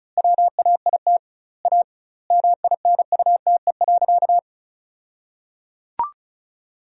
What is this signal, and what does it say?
Morse code, heard as a keyed tone, about 34 wpm, 700 Hz. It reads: WAIT A MINUTE.
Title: 2x Word-Spacing - Sentences from Top 500 Words - 35wpm